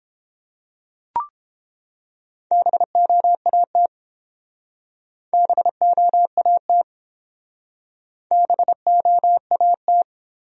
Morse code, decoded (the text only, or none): BOAT BOAT BOAT